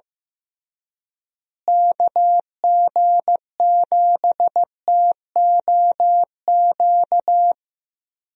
Morse code, heard as a keyed tone, about 15 wpm, 700 Hz. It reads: KG7TOQ
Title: USA Call Signs - 6 Characters - Repeated 3x - 15wpm